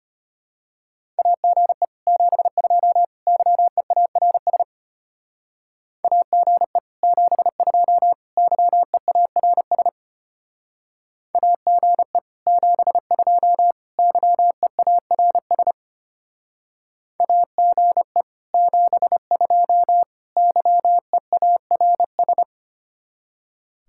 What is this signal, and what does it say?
Morse code, heard as a keyed tone, about 30 wpm, 700 Hz. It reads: AGE 72 YEARS AGE 72 YEARS AGE 72 YEARS AGE 72 YEARS
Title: Speed-Racing Common QSO Phrases - 25wpm